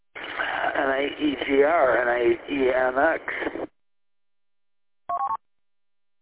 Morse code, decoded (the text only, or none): none